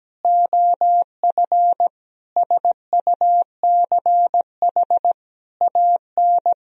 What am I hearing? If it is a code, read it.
Morse code, heard as a keyed tone, about 17 wpm, 700 Hz. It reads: OF SUCH AN
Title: Sentences from Top 400 Words and Abbreviations 17wpm